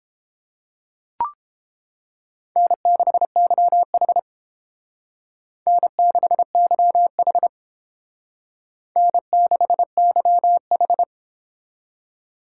Morse code, heard as a keyed tone, about 29 wpm, 700 Hz. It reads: N6YH N6YH N6YH